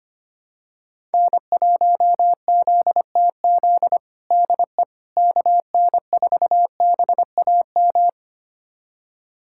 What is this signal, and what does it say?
Morse code, heard as a keyed tone, about 25 wpm, 700 Hz. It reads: N1ZTZ DE KN4BAM
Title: USA Callsign DE Callsign 25wpm